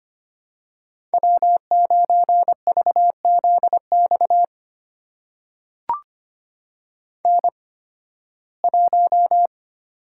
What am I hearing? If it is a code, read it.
Morse code, heard as a keyed tone, about 25 wpm, spaced as for 16 wpm, 700 Hz. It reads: W9VZX N1
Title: USA Call Signs - 8x Character Spacing - 25wpm